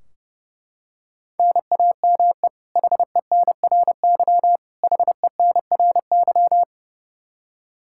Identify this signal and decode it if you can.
Morse code, heard as a keyed tone, about 32 wpm, 700 Hz. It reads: NAME HENRY HENRY
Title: Practice QSO - Exchange 1 - Beginning - 30wpm